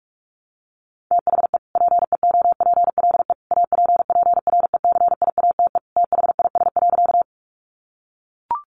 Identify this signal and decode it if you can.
Morse code, heard as a keyed tone, about 44 wpm, 700 Hz. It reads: THE PEOPLE APPRECIATE THIS.
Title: Sentences from Top 1300 Words 45wpm